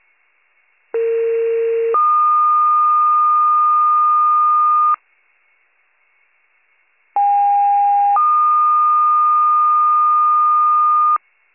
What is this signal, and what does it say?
Two-tone page, heard as a one-second tone, then a three-second tone, about 460 then 1220 Hz, about 810 then 1220 Hz. It is heard from 0.9 to 11.2 s.